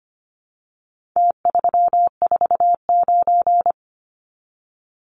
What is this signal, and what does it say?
Morse code, heard as a keyed tone, about 25 wpm, 700 Hz. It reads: T349
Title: Numbers - Contest Style - 4-Digit 25wpm